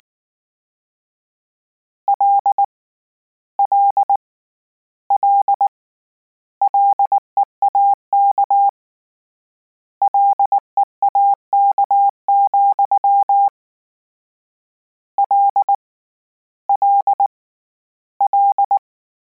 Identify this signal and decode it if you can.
Morse code, heard as a keyed tone, about 19 wpm, 800 Hz. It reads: L L L LEAK LEAK, L L L